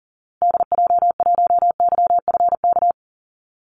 Morse code, heard as a keyed tone, about 40 wpm, 700 Hz. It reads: DJ1YFK
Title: Call Signs Repeated 2x - 40wpm